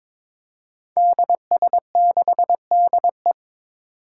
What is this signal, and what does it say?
Morse code, heard as a keyed tone, about 22 wpm, 700 Hz. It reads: DS6DE
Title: Five Challenging Letter-Number Combinations 22wpm